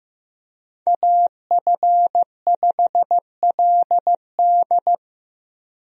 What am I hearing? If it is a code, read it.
Morse code, heard as a keyed tone, about 15 wpm, 700 Hz. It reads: AF5LD
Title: USA Call Signs - 5 Characters - 15wpm